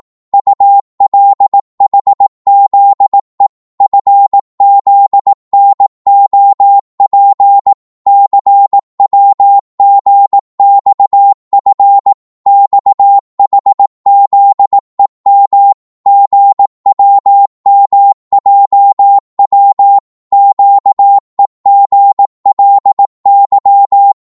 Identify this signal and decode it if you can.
Morse code, heard as a keyed tone, about 18 wpm, 800 Hz. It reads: ULHZE FZNOP CWGXF XHZEM GWMJW QEGLY
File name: Morse / ltr-18wpm